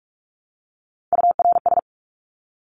Morse code, heard as a keyed tone, about 44 wpm, 700 Hz. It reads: URS
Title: Abbreviations 45wpm